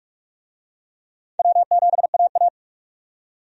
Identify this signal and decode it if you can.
Morse code, heard as a keyed tone, about 46 wpm, 700 Hz. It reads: W7AA
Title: USA Call Signs - 4 Characters - 45wpm